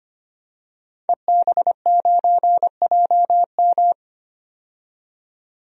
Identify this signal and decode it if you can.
Morse code, heard as a keyed tone, about 25 wpm, 700 Hz. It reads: EB9JM
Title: DXCC-100 Call Signs - 5 Characters - 25wpm